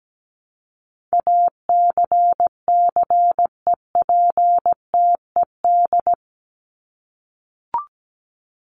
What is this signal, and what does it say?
Morse code, heard as a keyed tone, about 17 wpm, 700 Hz. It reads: ACCEPTED
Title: ED Suffix - Top 100 Words with Base Words 17wpm